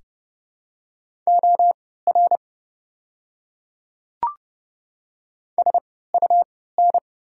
Morse code, heard as a keyed tone, about 30 wpm, spaced as for 17 wpm, 700 Hz. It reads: OR SUN
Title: 3x Character Spacing - Top 300 Words - 30wpm